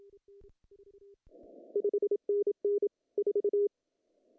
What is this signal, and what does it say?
Morse code, heard as a keyed tone, about 27 wpm, 410 Hz. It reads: 5NN 4
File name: CW / cqww-cw-k5tr